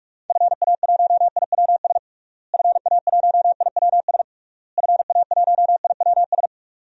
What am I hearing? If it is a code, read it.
Morse code, heard as a keyed tone, about 46 wpm, 700 Hz. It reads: FA1IWS FA1IWS FA1IWS
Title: DXCC-100 Call Signs - 6 Characters - Repeated 3x - 45wpm